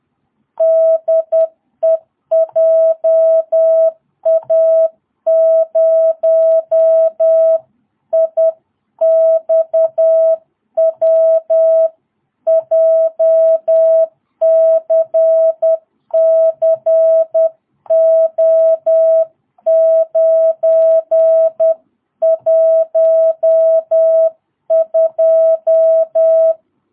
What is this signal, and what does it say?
Morse code, heard as a keyed tone, about 10 wpm, 660 Hz. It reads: DE JA0IXW JCCO912
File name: jh8cbh / 2022cw